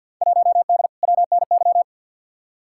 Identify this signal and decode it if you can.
Morse code, heard as a keyed tone, about 53 wpm, 700 Hz. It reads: 1D WNY
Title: Field Day Exchanges - 50wpm